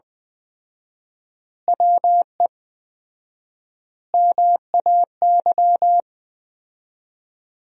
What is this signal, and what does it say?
Morse code, heard as a keyed tone, about 20 wpm, 700 Hz. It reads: WE MAY
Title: 4x Word-Spacing - Sentences from Top 300 Words - 20wpm